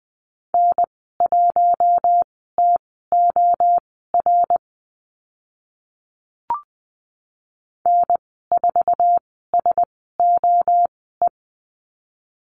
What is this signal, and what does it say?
Morse code, heard as a keyed tone, about 20 wpm, spaced as for 14 wpm, 700 Hz. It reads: N1TOR N4SOE
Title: 082 Lesson 14g - Number 2 - Call Signs 20wpm